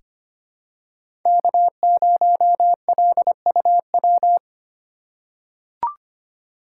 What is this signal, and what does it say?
Morse code, heard as a keyed tone, about 25 wpm, 700 Hz. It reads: K0LUW
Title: K1USN SST QSO Practice - 25wpm